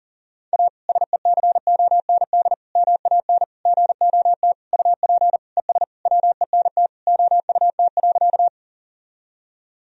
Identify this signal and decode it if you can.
Morse code, heard as a keyed tone, about 40 wpm, 700 Hz. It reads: A SECOND MAN GOT UP ES WENT OUT.
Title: Sentences from Top 300 Words and Abbreviations 40wpm